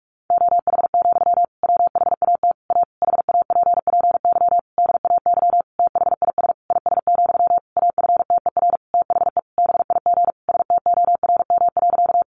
Morse code, heard as a keyed tone, about 44 wpm, 700 Hz. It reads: OH, WHAT A HAPPY DAY THIS IS, AFTER THE BIG STORM.
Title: Sentences from Top 1500 Words 45wpm